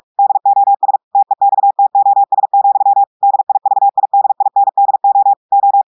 Morse code, heard as a keyed tone, about 46 wpm, 800 Hz. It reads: DOS TEXTOS, DIVIDINDO O
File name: audios / portugais / portugais texte en clair 1 vitesse 45 mots minutes